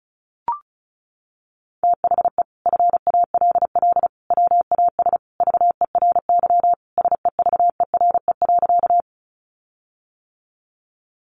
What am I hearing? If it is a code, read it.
Morse code, heard as a keyed tone, about 34 wpm, 700 Hz. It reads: THE FALL WAS VERY SEVERE.
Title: Sentences from Top 1400 Words 35wpm